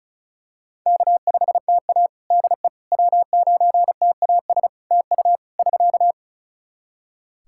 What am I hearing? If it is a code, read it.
Morse code, heard as a keyed tone, about 34 wpm, 700 Hz. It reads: K5TA DE W9TAS TU <SK>
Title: Practice QSO - Exchange 4 - Ending - 35wpm